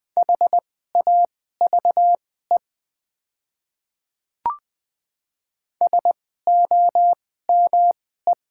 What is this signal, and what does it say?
Morse code, heard as a keyed tone, about 20 wpm, spaced as for 15 wpm, 700 Hz. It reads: HAVE SOME
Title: 2x Character Spacing - Top 100 Words - 20wpm